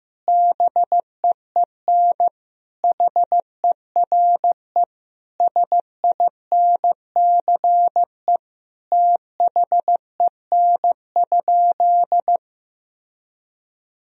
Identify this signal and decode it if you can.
Morse code, heard as a keyed tone, about 15 wpm, 700 Hz. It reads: BEEN HERE SINCE THEN?
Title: Sentences from Top 1000 Words - Review Part 1 - 15wpm